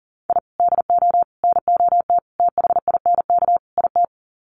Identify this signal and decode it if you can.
Morse code, heard as a keyed tone, about 40 wpm, 700 Hz. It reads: I DO NOT THINK IT
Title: Sets of 5 Words 40wpm